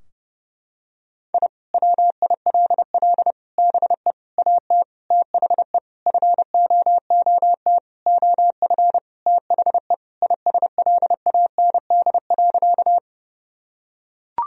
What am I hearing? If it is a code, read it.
Morse code, heard as a keyed tone, about 30 wpm, 700 Hz. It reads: I WILL BE AT THE FOOT OF THE ISLAND.
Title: Sentences from Top 1000 Words - Review Part 3 - 30wpm